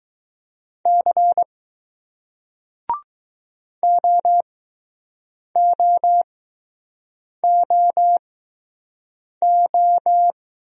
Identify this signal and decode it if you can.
Morse code, heard as a keyed tone, about 20 wpm, spaced as for 6 wpm, 700 Hz. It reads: C OOOO